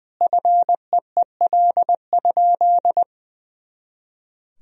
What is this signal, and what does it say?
Morse code, heard as a keyed tone, about 20 wpm, 700 Hz. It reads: FEEL?
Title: Sentences from Top 400 Words and Abbreviations 20wpm